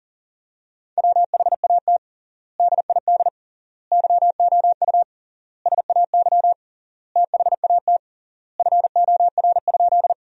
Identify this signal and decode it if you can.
Morse code, heard as a keyed tone, about 40 wpm, 700 Hz. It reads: WHAT DID YOU SAY THAT FOR?
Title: 3x Word-Spacing - Sentences from Top 200 Words - 40wpm